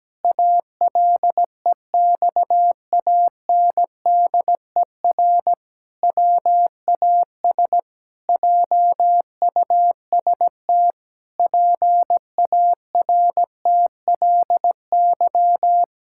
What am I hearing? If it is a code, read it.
Morse code, heard as a keyed tone, about 17 wpm, 700 Hz. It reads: ALEXANDER WAS JUST PARTLY
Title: Pangram Challenge v2 - 17wpm